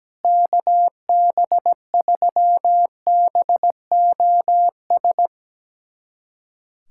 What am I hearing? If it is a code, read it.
Morse code, heard as a keyed tone, about 17 wpm, 700 Hz. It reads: KB3BOS
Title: USA Callsign DE Callsign 17wpm